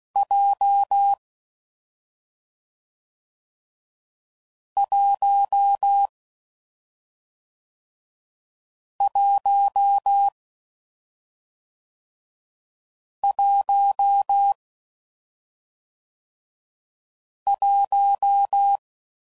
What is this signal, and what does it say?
Morse code, heard as a keyed tone, about 16 wpm, 800 Hz. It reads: J 1 1 1 1
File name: Combinations / Difficult Combos / AWJ1